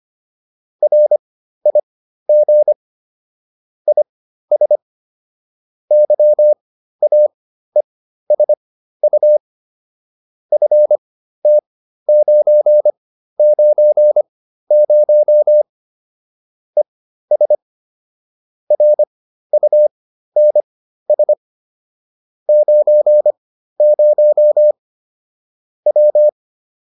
Morse code, heard as a keyed tone, about 25 wpm, spaced as for 13 wpm, 600 Hz. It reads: RIG IS YAESU FT990 ES RUNS 90 W